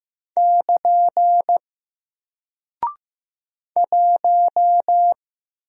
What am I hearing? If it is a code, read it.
Morse code, heard as a keyed tone, about 15 wpm, 700 Hz. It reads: <KN> 1
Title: Single Character (Letter:Number:Punctuation:Prosign) Alt 15wpm